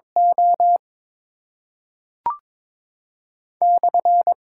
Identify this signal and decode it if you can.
Morse code, heard as a keyed tone, about 22 wpm, 700 Hz. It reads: O /
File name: Top 90 QSO Elements - 6x Character Spacing - 22wpm